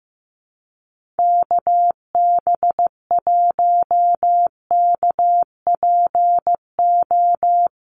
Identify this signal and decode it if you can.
Morse code, heard as a keyed tone, about 15 wpm, 700 Hz. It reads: KB1KPO
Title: USA Call Signs 15wpm